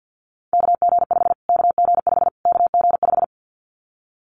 Morse code, heard as a keyed tone, about 50 wpm, 700 Hz. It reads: KG5 KG5 KG5
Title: Callsign Prefixes for Top 25 DXCC - Repeated 3x - 50wpm